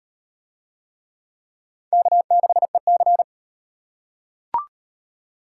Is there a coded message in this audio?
Yes: Morse code from 1.9 to 3.2 s, heard as a keyed tone, about 34 wpm, 700 Hz. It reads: K6EC